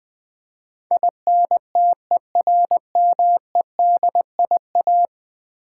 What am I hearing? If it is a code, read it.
Morse code, heard as a keyed tone, about 20 wpm, 700 Hz. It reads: INTERMEDIA
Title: Build A Word - 5 Letters At A Time - 12-Letter Words - 20wpm